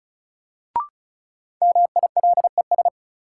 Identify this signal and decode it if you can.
Morse code, heard as a keyed tone, about 36 wpm, 700 Hz. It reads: MILES
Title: Sentences from Top 1000 Words - Review Part 3 - New Word Drill - 35wpm